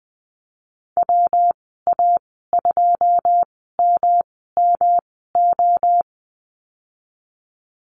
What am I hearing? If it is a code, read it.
Morse code, heard as a keyed tone, about 20 wpm, spaced as for 15 wpm, 700 Hz. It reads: WA2MMO
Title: 173 Lesson 28g - Number 9 - Call Signs 20wpm